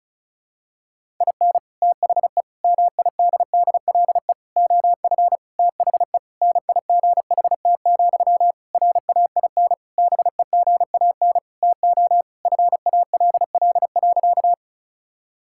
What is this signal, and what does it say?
Morse code, heard as a keyed tone, about 34 wpm, 700 Hz. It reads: IN THE MIDDLE OF THE NIGHT, RAIN BEGAN TO FALL.
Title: Sentences from Top 1000 Words - Review Part 6 - 35wpm